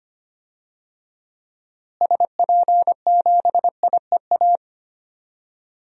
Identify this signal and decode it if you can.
Morse code, heard as a keyed tone, about 25 wpm, 700 Hz. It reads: SP7IEA
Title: DXCC-100 Call Signs - 6 Characters - Repeated 3x - 25wpm